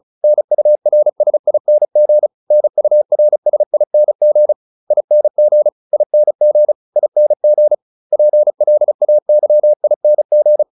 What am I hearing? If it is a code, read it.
Morse code, heard as a keyed tone, about 35 wpm, 600 Hz. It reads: NURSING NURSING ING ING ING PLAYING